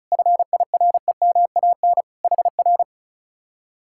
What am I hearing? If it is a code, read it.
Morse code, heard as a keyed tone, about 44 wpm, 700 Hz. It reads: FIREMAN HR
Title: Speed-Racing Common QSO Phrases - 35wpm